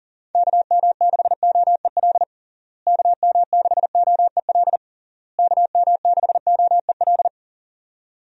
Morse code, heard as a keyed tone, about 40 wpm, 700 Hz. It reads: KM6OEL KM6OEL KM6OEL
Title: USA Call Signs - 6 Characters - Repeated 3x - 40wpm